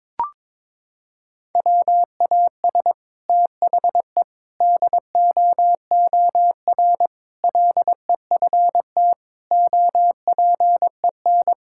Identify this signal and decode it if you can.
Morse code, heard as a keyed tone, about 22 wpm, 700 Hz. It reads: WAS THE DOOR LEFT OPEN
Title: Sentences from Top 1000 Words - Review Part 6 - 22wpm